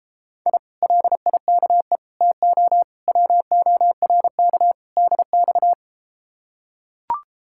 Morse code, heard as a keyed tone, about 33 wpm, 700 Hz. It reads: I LIKE TO WORK DX